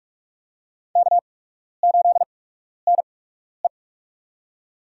Morse code, heard as a keyed tone, about 43 wpm, spaced as for 12 wpm, 700 Hz. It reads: K8NE
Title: USA Call Signs - 8x Character Spacing - 45wpm